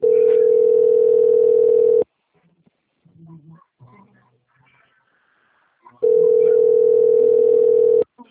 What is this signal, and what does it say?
Ringback tone (North America), 440 + 480 Hz, with speech.
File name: out- - - - - .40689